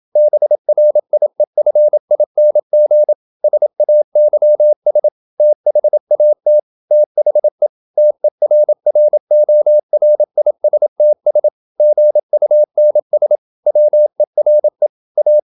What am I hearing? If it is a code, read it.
Morse code, heard as a keyed tone, about 27 wpm, 600 Hz. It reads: BRIEFING SAYS THAT THE TERRORISTS GUNS WERE A
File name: Never 0027wpm